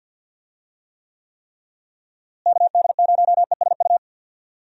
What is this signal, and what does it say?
Morse code, heard as a keyed tone, about 52 wpm, 700 Hz. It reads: KD0ESU
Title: USA Call Signs - 6 Characters - Repeated 3x - 50wpm